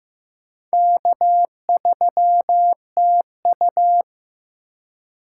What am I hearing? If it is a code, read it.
Morse code, heard as a keyed tone, about 15 wpm, 700 Hz. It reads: K3TU
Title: USA Call Signs - 4 Characters - 3x Character Spacing - 15wpm